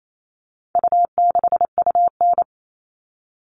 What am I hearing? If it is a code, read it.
Morse code, heard as a keyed tone, about 28 wpm, 700 Hz. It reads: U6UN
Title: All Cut Numbers - 4-Digit 28wpm